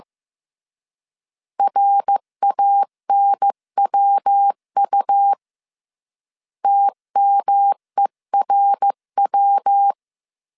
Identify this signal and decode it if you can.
Morse code, heard as a keyed tone, about 15 wpm, 800 Hz. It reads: RANWU TMERW
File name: CW / DET / 1 3 1500C S C 14wpm 2458